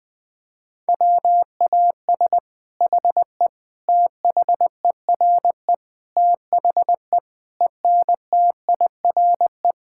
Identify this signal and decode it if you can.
Morse code, heard as a keyed tone, about 20 wpm, 700 Hz. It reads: WAS HE THERE THE ENTIRE
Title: Sentences from Top 1000 Words - Review Part 5 - 20wpm